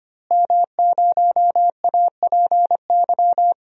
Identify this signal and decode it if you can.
Morse code, heard as a keyed tone, about 25 wpm, 700 Hz. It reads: M0APY
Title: DXCC-100 Call Signs - 5 Characters - Repeated 2x - 25wpm